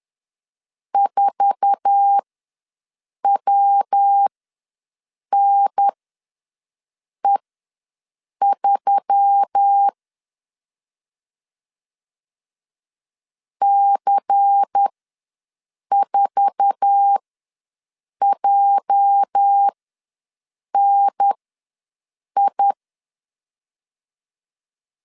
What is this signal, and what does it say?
Morse code, heard as a keyed tone, about 10 wpm, spaced as for 6 wpm, 800 Hz. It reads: 4WNE3 C4JNI
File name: LEZIONI / 8 YJ45 / 8 YJ4 ws 10wpm 4020